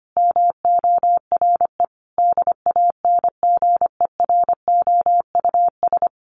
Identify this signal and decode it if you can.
Morse code, heard as a keyed tone, about 25 wpm, 700 Hz. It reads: MORE DANGEROUS
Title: Sentences from Top 1400 Words 25wpm